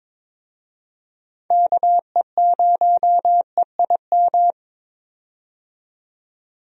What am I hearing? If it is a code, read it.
Morse code, heard as a keyed tone, about 22 wpm, 700 Hz. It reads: KE0EIM
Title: USA Call Signs - 6 Characters - 22wpm